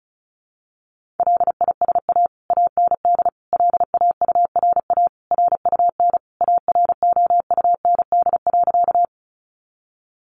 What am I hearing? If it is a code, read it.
Morse code, heard as a keyed tone, about 34 wpm, 700 Hz. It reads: LISA AND LAURA RUN AROUND.
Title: Additional Easy Phrases 35wpm